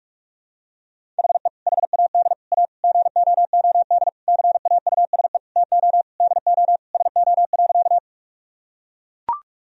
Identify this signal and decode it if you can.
Morse code, heard as a keyed tone, about 41 wpm, 700 Hz. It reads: HE HAD A GOOD CAUSE TO DO SO.
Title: Sentences from Top 1000 Words - Review Part 1 - 45wpm